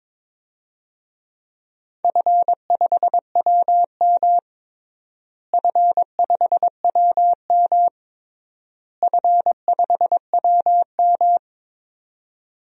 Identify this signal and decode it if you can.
Morse code, heard as a keyed tone, about 22 wpm, 700 Hz. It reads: F5WM F5WM F5WM
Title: DXCC-100 Call Signs - 4 Characters - Repeated 3x - 22wpm